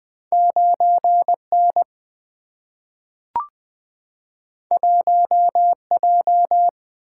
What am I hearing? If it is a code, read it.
Morse code, heard as a keyed tone, about 20 wpm, 700 Hz. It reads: 9 N 1 J
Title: Single-Letter Single-Number 20wpm